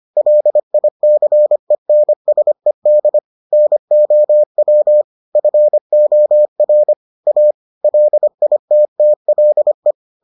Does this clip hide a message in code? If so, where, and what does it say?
Yes: Morse code from 0.2 to 9.9 s, heard as a keyed tone, about 25 wpm, 600 Hz. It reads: LICENSED NOW FOR A LITTLE